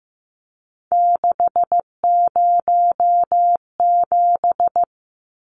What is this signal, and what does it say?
Morse code, heard as a keyed tone, about 15 wpm, 700 Hz. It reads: 607
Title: Numbers - 3-Digit 15wpm